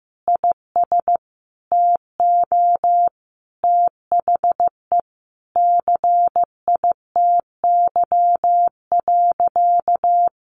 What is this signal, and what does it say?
Morse code, heard as a keyed tone, about 15 wpm, 700 Hz. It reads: IS TO THE CITY.
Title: Sentences from Top 1200 Words 15wpm